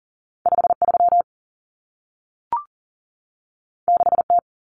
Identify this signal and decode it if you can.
Morse code, heard as a keyed tone, about 40 wpm, 700 Hz. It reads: 53 6T
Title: Common Cut Numbers - 2-Digit 40wpm